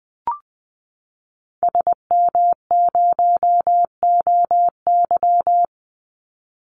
Morse code, heard as a keyed tone, about 20 wpm, 700 Hz. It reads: SM0OY
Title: Call Signs Repeated 3x - 20wpm